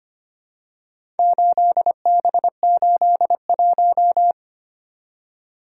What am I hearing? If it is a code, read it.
Morse code, heard as a keyed tone, about 25 wpm, 700 Hz. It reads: 8B81
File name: practice / 25wpm / Four Letter-Number Combinations - 25wpm